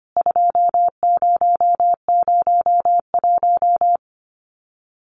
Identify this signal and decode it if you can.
Morse code, heard as a keyed tone, about 25 wpm, 700 Hz. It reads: 2001
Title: CWT Exchange 25wpm